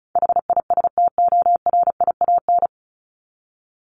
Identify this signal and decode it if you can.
Morse code, heard as a keyed tone, about 34 wpm, 700 Hz. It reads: HISTORIAN
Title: Top 1751-2000 Words 35wpm